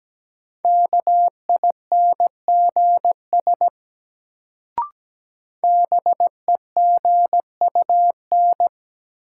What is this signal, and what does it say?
Morse code, heard as a keyed tone, about 17 wpm, 700 Hz. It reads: KINGS BEGUN
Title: Sentences from Top 1000 Words - Review Part 6 - New Word Drill - 17wpm